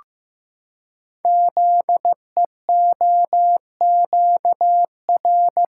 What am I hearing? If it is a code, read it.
Morse code, heard as a keyed tone, about 15 wpm, 700 Hz. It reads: ZEOQR
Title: Code Groups - 5 Letters - 15wpm